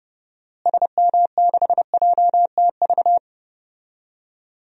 Morse code, heard as a keyed tone, about 32 wpm, 700 Hz. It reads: SM6JTV
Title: DXCC-100 Call Signs - 6 Characters - Repeated 2x - 30wpm